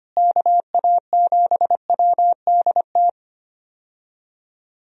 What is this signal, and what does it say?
Morse code, heard as a keyed tone, about 25 wpm, 700 Hz. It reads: KA7WDT
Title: USA Call Signs - 6 Characters - Repeated 3x - 25wpm